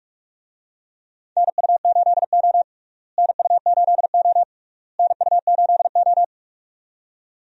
Morse code, heard as a keyed tone, about 46 wpm, 700 Hz. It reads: NU8O NU8O NU8O
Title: USA Call Signs - 4 Characters - Repeated 3x - 45wpm